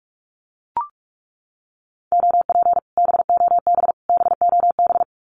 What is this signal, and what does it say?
Morse code, heard as a keyed tone, about 44 wpm, 700 Hz. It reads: OP BOB BOB
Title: Common QSO Phrases Encore 45wpm